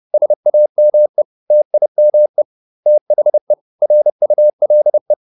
Morse code, heard as a keyed tone, about 30 wpm, 600 Hz. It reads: SAME TIME THE RULE